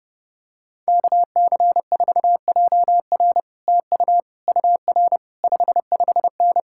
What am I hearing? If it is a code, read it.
Morse code, heard as a keyed tone, about 30 wpm, 700 Hz. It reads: KC4JR TU UR 55N